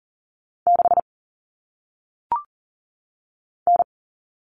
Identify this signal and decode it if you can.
Morse code, heard as a keyed tone, about 40 wpm, 700 Hz. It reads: 6 N
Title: All Cut Numbers - 1-Digit 40wpm